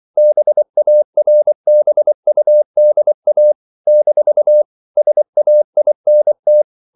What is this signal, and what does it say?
Morse code, heard as a keyed tone, about 24 wpm, 600 Hz. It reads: BARBUDA - SAINT
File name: CW / Practice / NationalCapitals24